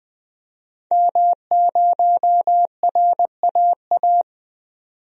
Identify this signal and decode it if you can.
Morse code, heard as a keyed tone, about 20 wpm, 700 Hz. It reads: M0RAA
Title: DXCC-100 Call Signs - 5 Characters - 20wpm